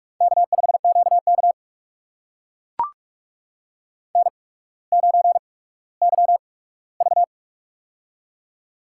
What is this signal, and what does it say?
Morse code, heard as a keyed tone, about 44 wpm, spaced as for 19 wpm, 700 Hz. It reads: K5QK N9YV
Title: USA Call Signs - 8x Character Spacing - 45wpm